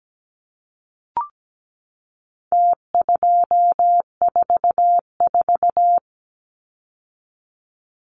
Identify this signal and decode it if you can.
Morse code, heard as a keyed tone, about 17 wpm, 700 Hz. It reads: T244
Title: Numbers - Contest Style - 4-Digit 17wpm